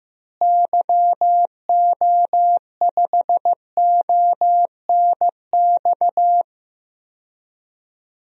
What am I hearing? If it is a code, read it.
Morse code, heard as a keyed tone, about 15 wpm, 700 Hz. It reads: YO5ONX
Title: DXCC-100 Call Signs - 6 Characters - Repeated 2x - 15wpm